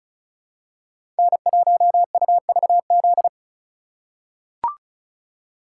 Morse code, heard as a keyed tone, about 34 wpm, 700 Hz. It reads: N1UVZ
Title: USA Call Signs - 5 Characters - 4x Character Spacing - 35wpm